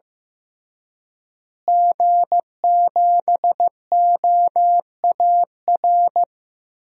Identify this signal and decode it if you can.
Morse code, heard as a keyed tone, about 15 wpm, 700 Hz. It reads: G7OAR
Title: DXCC-100 Call Signs - 5 Characters - Repeated 3x - 15wpm